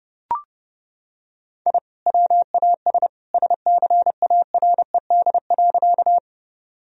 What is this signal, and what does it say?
Morse code, heard as a keyed tone, about 30 wpm, 700 Hz. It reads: I WAS SCARED.
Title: Sentences from Top 1000 Words - Review Part 4 - 30wpm